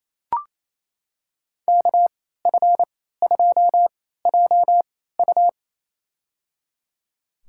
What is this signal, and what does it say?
Morse code, heard as a keyed tone, about 28 wpm, spaced as for 16 wpm, 700 Hz. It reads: KF2JU
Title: USA Call Signs - 3x Character Spacing - 28wpm